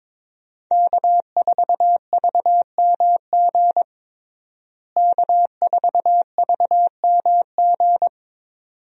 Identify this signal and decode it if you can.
Morse code, heard as a keyed tone, about 22 wpm, 700 Hz. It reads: K4VMG K4VMG
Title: USA Call Signs - 5 Characters - Repeated 2x - 22wpm